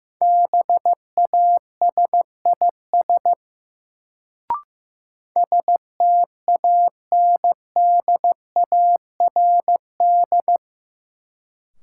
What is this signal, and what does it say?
Morse code, heard as a keyed tone, about 16 wpm, 700 Hz. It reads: BASIS STANDARD
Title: Sentences from Top 1200 Words - New Word Drill Alt - 15wpm